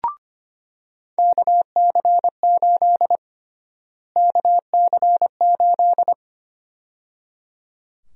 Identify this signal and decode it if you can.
Morse code, heard as a keyed tone, about 25 wpm, 700 Hz. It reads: KC8 KC8
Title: USA Call Sign Prefixes - Repeated 2x - 25wpm